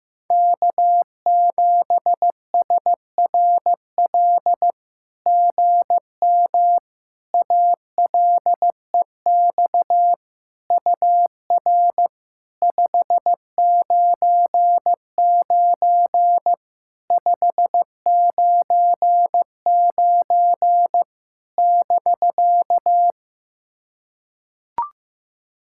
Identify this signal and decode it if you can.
Morse code, heard as a keyed tone, about 15 wpm, 700 Hz. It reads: K7SRL GM ALEX UR 599 599 <BK>
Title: SOTA QSO Practice 15wpm